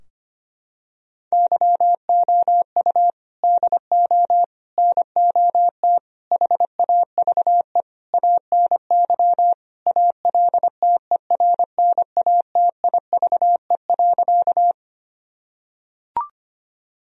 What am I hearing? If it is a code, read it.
Morse code, heard as a keyed tone, about 25 wpm, 700 Hz. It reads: YOU DO NOT HAVE ANY ALTERNATIVE.
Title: Sentences from Top 1000 Words - Review Part 4 - 25wpm